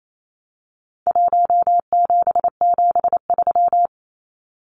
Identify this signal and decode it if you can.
Morse code, heard as a keyed tone, about 28 wpm, 700 Hz. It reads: 1773
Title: Numbers - Contest Style - 4-Digit 28wpm